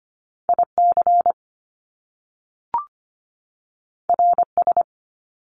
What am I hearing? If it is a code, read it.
Morse code, heard as a keyed tone, about 25 wpm, 700 Hz. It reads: IC RS
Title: Top Two Letter Combinations in Words 25wpm